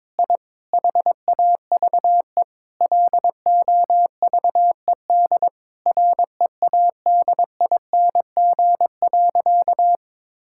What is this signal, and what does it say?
Morse code, heard as a keyed tone, about 22 wpm, 700 Hz. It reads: I HAVE LOVED READING.
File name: practice / 22wpm / Sentences with 8-Word Phrases - 22wpm